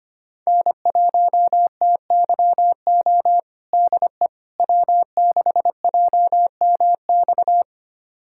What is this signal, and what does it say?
Morse code, heard as a keyed tone, about 25 wpm, 700 Hz. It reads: N1TYO DE W6JMX
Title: USA Callsign DE Callsign - 5 Characters - 25wpm